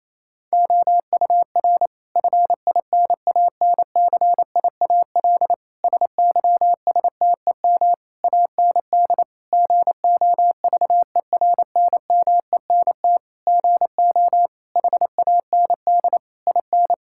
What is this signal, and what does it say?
Morse code, heard as a keyed tone, about 28 wpm, 700 Hz. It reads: OUR FINANCIAL SYSTEM AND GOVERNMENT GO HAND IN
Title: Sentences from Top 1000 Words - Review Part 5 - 28wpm